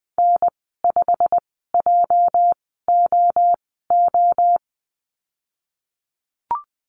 Morse code, heard as a keyed tone, about 20 wpm, spaced as for 15 wpm, 700 Hz. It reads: N5JOO
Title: 207 Lesson 33g - Letter j - Call Signs 20wpm